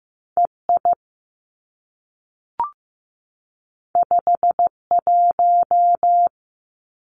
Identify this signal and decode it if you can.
Morse code, heard as a keyed tone, about 15 wpm, 700 Hz. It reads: EI 51
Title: Two Challenging Letter-Number Combinations 15wpm